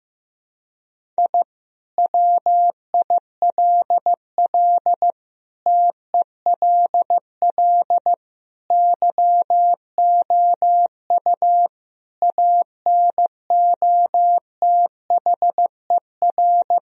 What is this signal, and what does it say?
Morse code, heard as a keyed tone, about 15 wpm, 700 Hz. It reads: I WILL TELL YOU ANOTHER
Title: Sentences from Top 1000 Words - Review Part 1 - 15wpm